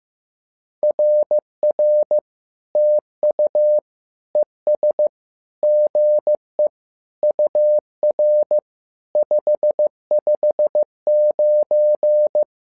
Morse code, heard as a keyed tone, about 15 wpm, 600 Hz. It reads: RR TU ES GE UR 559
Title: SOTA QSO Practice 15wpm